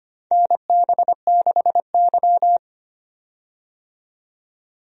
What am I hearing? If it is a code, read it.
Morse code, heard as a keyed tone, about 25 wpm, 700 Hz. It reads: NB6Y
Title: USA Call Signs - 4 Characters - 25wpm